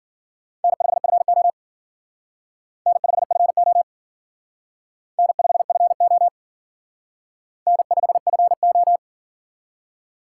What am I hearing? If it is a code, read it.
Morse code, heard as a keyed tone, about 50 wpm, 700 Hz. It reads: N5FO N5FO N5FO N5FO
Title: Speed Racing - USA Call Signs 40wpm